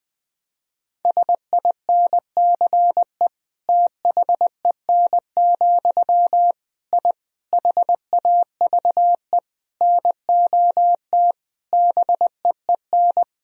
Morse code, heard as a keyed tone, about 20 wpm, 700 Hz. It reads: SINCE THEN, I HAVE NOT BEEN
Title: Sentences from Top 1000 Words - Review Part 1 - 20wpm